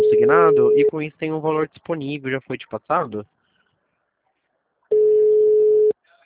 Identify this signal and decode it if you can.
Ringback tone (Europe, 425 Hz), with speech.
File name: out- -2010- - - .99184